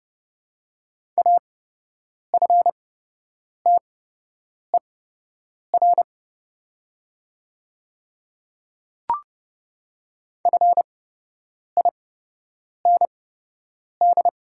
Morse code, heard as a keyed tone, about 30 wpm, spaced as for 8 wpm, 700 Hz. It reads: AFTER FIND
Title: 8x Character Spacing - Top 200 Words - 30wpm